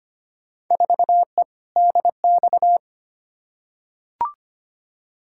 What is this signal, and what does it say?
Morse code, heard as a keyed tone, about 25 wpm, 700 Hz. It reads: 4E DX
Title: Field Day Exchanges Encore - 25wpm